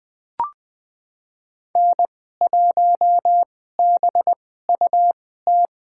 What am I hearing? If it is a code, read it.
Morse code, heard as a keyed tone, about 21 wpm, spaced as for 15 wpm, 700 Hz. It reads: N1BUT
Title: USA Call Signs - 5 Characters - 2x Character Spacing - 20wpm